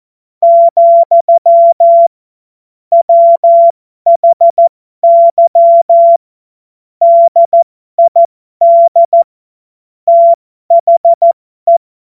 Morse code, heard as a keyed tone, about 14 wpm, spaced as for 12 wpm, 680 Hz. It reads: , WHY DID THE
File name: Morsekurs / Witze